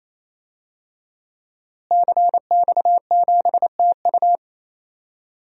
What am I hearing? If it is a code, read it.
Morse code, heard as a keyed tone, about 28 wpm, 700 Hz. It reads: CX7TU
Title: DXCC-100 Call Signs - 5 Characters - Repeated 3x - 28wpm